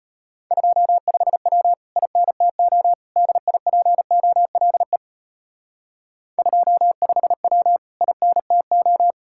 Morse code, heard as a keyed tone, about 31 wpm, 700 Hz. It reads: 25W INTO DIPOLE 25W INTO
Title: Speed-Racing Common QSO Phrases - 25wpm